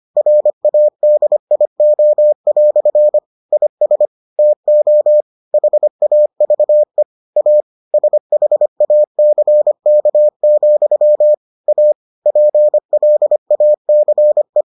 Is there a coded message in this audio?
Yes: Morse code from 0.2 to 14.6 s, heard as a keyed tone, about 25 wpm, 600 Hz. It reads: RADIO" IS TO HAVE A SHACK, A PLACE